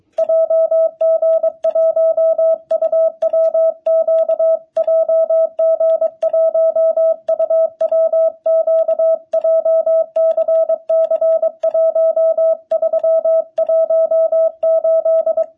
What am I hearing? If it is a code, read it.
Morse code, heard as a keyed tone, about 24 wpm, 660 Hz. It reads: JG1UWQ JG1UWQ JCC1318
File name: jh8cbh / 2021cw